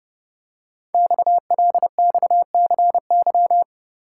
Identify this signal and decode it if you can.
Morse code, heard as a keyed tone, about 30 wpm, 700 Hz. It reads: XLXCY
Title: Code Groups - 5 Letters - 30wpm